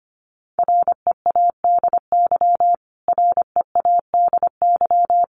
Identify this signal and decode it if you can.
Morse code, heard as a keyed tone, about 25 wpm, 700 Hz. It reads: READY READY
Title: Top 500 Words Repeated 25wpm